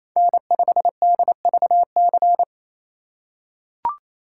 Morse code, heard as a keyed tone, about 28 wpm, 700 Hz. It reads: N5DVC
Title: USA Call Signs - 2x Character Spacing - 28wpm